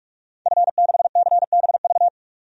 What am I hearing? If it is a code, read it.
Morse code, heard as a keyed tone, about 46 wpm, 700 Hz. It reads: F6CBV
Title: DXCC-100 Call Signs - 5 Characters - 45wpm